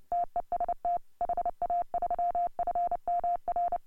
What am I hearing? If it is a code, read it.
Morse code, heard as a keyed tone, about 30 wpm, 690 Hz. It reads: TEST HA3FMR
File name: AUDIO2018 / HA3FMR CW